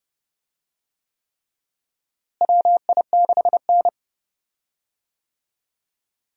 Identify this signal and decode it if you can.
Morse code, heard as a keyed tone, about 30 wpm, 700 Hz. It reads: WI6N
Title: USA Call Signs - 4 Characters - Repeated 3x - 30wpm